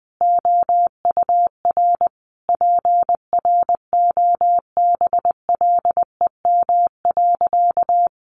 Morse code, heard as a keyed tone, about 19 wpm, 700 Hz. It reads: OUR PROBLEM.
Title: Sentences from Top 1300 Words 20wpm